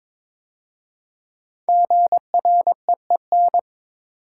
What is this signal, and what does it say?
Morse code, heard as a keyed tone, about 22 wpm, 700 Hz. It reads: GREEN
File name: 5 Letter Words - First 3 Letters Given - 22wpm